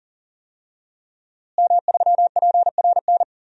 Morse code, heard as a keyed tone, about 39 wpm, 700 Hz. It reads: M3PRN
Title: DXCC-100 Call Signs - 5 Characters - Repeated 2x - 40wpm